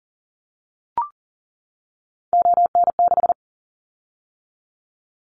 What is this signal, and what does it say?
Morse code, heard as a keyed tone, about 40 wpm, 700 Hz. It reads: ON6
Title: Callsign Prefixes for Top 25 DXCC - 40wpm